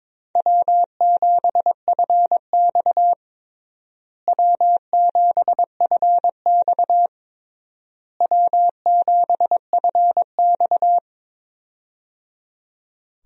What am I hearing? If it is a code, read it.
Morse code, heard as a keyed tone, about 22 wpm, 700 Hz. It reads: W7FX W7FX W7FX
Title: USA Call Signs - 4 Characters - Repeated 3x - 22wpm